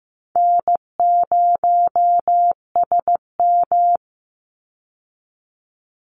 Morse code, heard as a keyed tone, about 15 wpm, 700 Hz. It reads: N0SM
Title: Easy Call Signs Repeated 3x - 15wpm